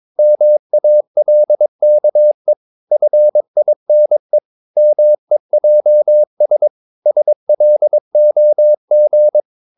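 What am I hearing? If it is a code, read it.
Morse code, heard as a keyed tone, about 22 wpm, 600 Hz. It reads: MALKE FINE MEJS SLOG
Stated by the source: Morse practice of short Danish words at 22 wpm, from CW med Gnister.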